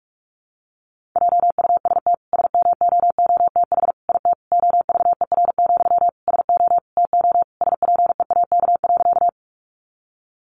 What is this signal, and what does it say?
Morse code, heard as a keyed tone, about 44 wpm, 700 Hz. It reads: JUST SMOOTH IT OVER, SO TO SPEAK.
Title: Sentences from Top 1400 Words 45wpm